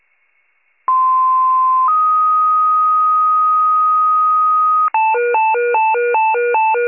Two-tone page, heard as a one-second tone, then a three-second tone, about 1040 then 1250 Hz. It is heard from 0.9 to 4.9 s.